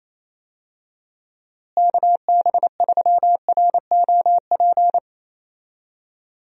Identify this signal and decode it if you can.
Morse code, heard as a keyed tone, about 28 wpm, 700 Hz. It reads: KB3ROP
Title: USA Call Signs - 6 Characters - Repeated 3x - 28wpm